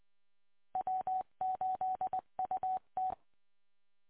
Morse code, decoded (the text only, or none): W8UT